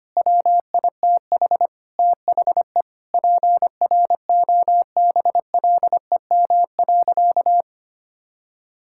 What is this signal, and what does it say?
Morse code, heard as a keyed tone, about 25 wpm, 700 Hz. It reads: WITH THE PROBLEM.
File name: practice / 25wpm / Sentences with 4-Word Phrases - 25wpm